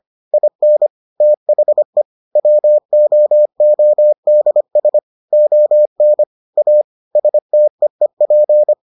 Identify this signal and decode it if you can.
Morse code, heard as a keyed tone, about 25 wpm, 600 Hz. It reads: IN THE WOODS ON A STEEP